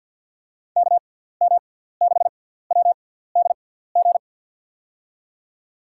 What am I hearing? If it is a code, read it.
Morse code, heard as a keyed tone, about 53 wpm, spaced as for 17 wpm, 700 Hz. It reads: KM6WDG
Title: USA Call Signs - 6 Characters - 6x Character Spacing - 50wpm